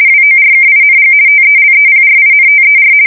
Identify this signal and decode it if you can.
RTTY, about 50 baud. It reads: WORKING TELEPRINTER
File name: fsk170-50B / teletype-corp